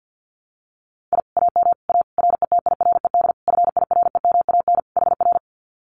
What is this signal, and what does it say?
Morse code, heard as a keyed tone, about 50 wpm, 700 Hz. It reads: I AM A RETIRED FIREMAN HR
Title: Common QSO Phrases Encore 50wpm